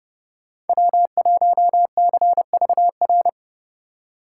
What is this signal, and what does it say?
Morse code, heard as a keyed tone, about 30 wpm, 700 Hz. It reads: W1CVR
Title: USA Call Signs - 6x Character Spacing - 30wpm